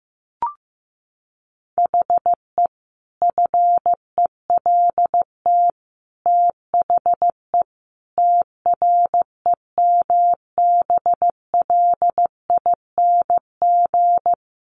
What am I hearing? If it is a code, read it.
Morse code, heard as a keyed tone, about 14 wpm, 700 Hz. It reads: HE FELT THE TREMBLING
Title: A Sound Of Thunder 15wpm